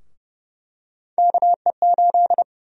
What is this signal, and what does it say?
Morse code, heard as a keyed tone, about 30 wpm, 700 Hz. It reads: KE8
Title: USA Call Sign Prefixes - 30wpm